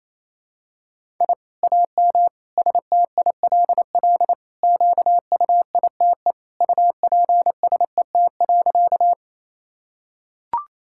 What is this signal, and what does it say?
Morse code, heard as a keyed tone, about 28 wpm, 700 Hz. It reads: I AM STILL QUITE UPSET.
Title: Sentences from Top 1000 Words - Review Part 5 - 28wpm